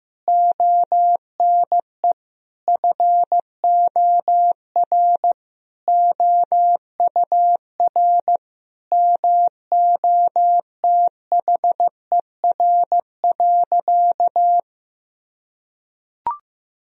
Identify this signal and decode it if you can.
Morse code, heard as a keyed tone, about 15 wpm, 700 Hz. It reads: ONE FOR OUR MOTHER.
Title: Sentences from Top 1000 Words - Review Part 2 - 15wpm